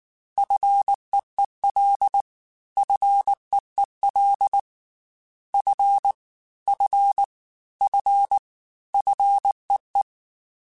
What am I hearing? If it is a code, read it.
Morse code, heard as a keyed tone, about 19 wpm, 800 Hz. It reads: FEEL FEEL F F F FEE
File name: F and L / F and L mp3 / F and L Prefixes 3x2 Twice @18